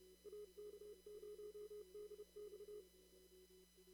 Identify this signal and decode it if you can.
Morse code, heard as a keyed tone, about 30 wpm, 420 Hz. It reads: AK0DX
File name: CW / kt5j wpx cw-op-k5pi